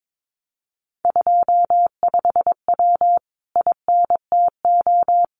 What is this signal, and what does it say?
Morse code, heard as a keyed tone, about 22 wpm, 700 Hz. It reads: 25W INTO
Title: Common QSO Phrases Encore 22wpm